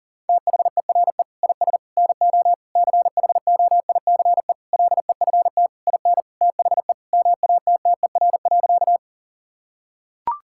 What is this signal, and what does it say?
Morse code, heard as a keyed tone, about 40 wpm, 700 Hz. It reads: THERE IS NO CHOICE LEFT IN THE MATTER.
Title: Sentences from Top 1000 Words - Review Part 6 - 40wpm